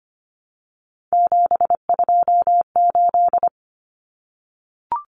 Morse code, heard as a keyed tone, about 25 wpm, 700 Hz. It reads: 728
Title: Common Cut Numbers - 3-Digit 25wpm